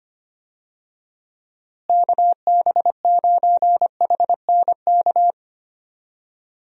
Morse code, heard as a keyed tone, about 25 wpm, 700 Hz. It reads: KB9HNK
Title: USA Call Signs - 6 Characters - Repeated 3x - 25wpm